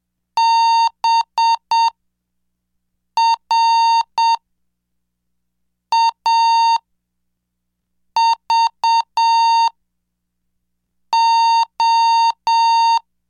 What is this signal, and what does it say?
Morse code, heard as a keyed tone, about 7 wpm, spaced as for 4 wpm, 920 Hz. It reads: BRAVO
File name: cursos / CDestudo / 11 Track 11